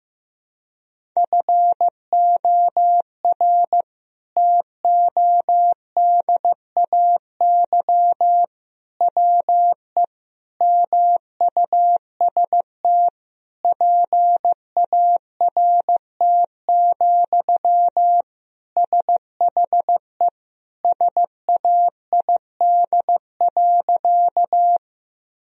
Morse code, heard as a keyed tone, about 15 wpm, 700 Hz. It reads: FOR TODAY WE MUST PART, SHE SAID.
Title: Sentences from Top 1000 Words - Review Part 1 - 15wpm